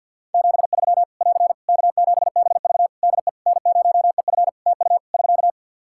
Morse code, heard as a keyed tone, about 54 wpm, 700 Hz. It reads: 73 <AR> K7BV DE N0EF TU <SK>
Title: Practice QSO - Exchange 4 - Ending - 50wpm